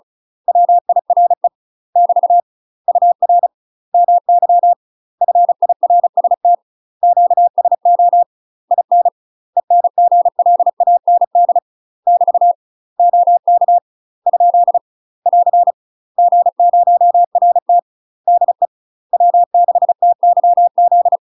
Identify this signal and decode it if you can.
Morse code, heard as a keyed tone, about 36 wpm, 700 Hz. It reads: WIRE <BT> UR MY FIRST QSO IN ENGLAND <BT> OK ? <AR> G0RT DE W6TYZ
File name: UK MASTER QSOs TRAINING / 35WPM / UK MASTER QSOs 0026